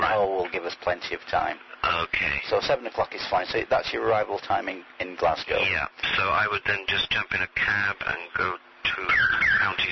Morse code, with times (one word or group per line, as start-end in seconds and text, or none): none